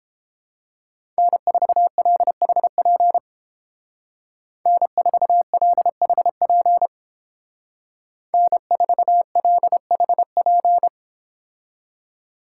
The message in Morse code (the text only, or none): N4LHP N4LHP N4LHP